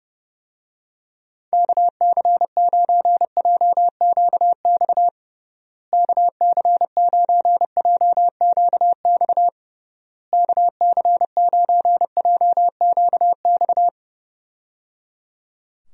Morse code, heard as a keyed tone, about 30 wpm, 700 Hz. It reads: KC9JQX KC9JQX KC9JQX
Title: USA Call Signs - 6 Characters - Repeated 3x - 30wpm